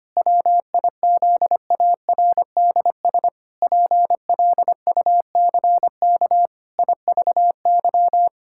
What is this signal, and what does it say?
Morse code, heard as a keyed tone, about 25 wpm, 700 Hz. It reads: WIZARDS PLUCK IVY
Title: Pangram Challenge v2 - 25wpm